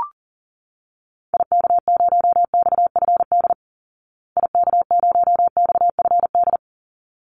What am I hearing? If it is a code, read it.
Morse code, heard as a keyed tone, about 40 wpm, 700 Hz. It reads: IK0XFD IK0XFD
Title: Call Signs Repeated 3x - 40wpm